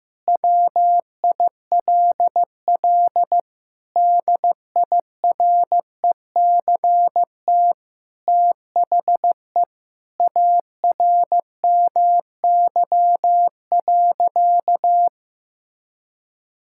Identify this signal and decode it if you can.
Morse code, heard as a keyed tone, about 15 wpm, 700 Hz. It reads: WILL DIRECT THE ARMY.
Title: Sentences from Top 1000 Words - Review Part 5 - 15wpm